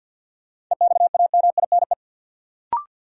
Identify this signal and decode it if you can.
Morse code, heard as a keyed tone, about 52 wpm, 700 Hz. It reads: EXAMINE
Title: Sentences from Top 1600 Words - New Word Drill - 50wpm